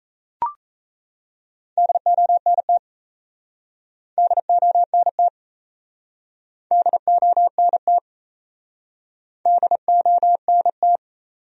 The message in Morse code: DONT DONT DONT DONT

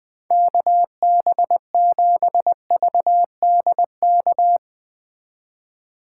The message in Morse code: KB7VDK

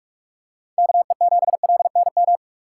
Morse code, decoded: KE7LNM

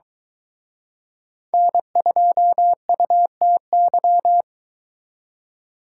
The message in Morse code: N2UTY

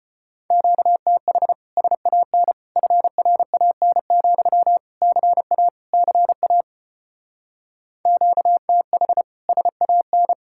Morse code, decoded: QTH SAN FRAN, CA CA QTH SAN

